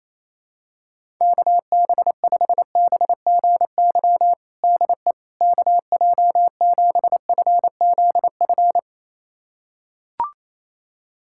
Morse code, heard as a keyed tone, about 28 wpm, 700 Hz. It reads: KB5BGY DE KJ7FZF
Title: USA Callsign DE Callsign 28wpm